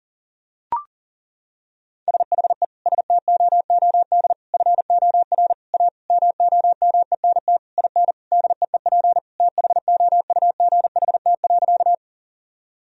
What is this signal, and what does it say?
Morse code, heard as a keyed tone, about 37 wpm, 700 Hz. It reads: SHE STOOD FOR A MOMENT IN DEEP THOUGHT.
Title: Sentences from Top 1000 Words - Review Part 3 - 40wpm